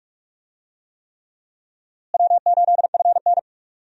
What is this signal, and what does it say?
Morse code, heard as a keyed tone, about 45 wpm, 700 Hz. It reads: W8FN
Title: USA Call Signs - 4 Characters - Repeated 3x - 45wpm